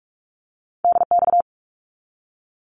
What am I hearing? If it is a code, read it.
Morse code, heard as a keyed tone, about 44 wpm, 700 Hz. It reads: DX
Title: Abbreviations 45wpm